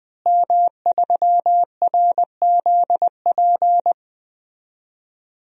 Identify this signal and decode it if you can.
Morse code, heard as a keyed tone, about 21 wpm, 700 Hz. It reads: M3RZP
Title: DXCC-100 Call Signs - 5 Characters - Repeated 2x - 20wpm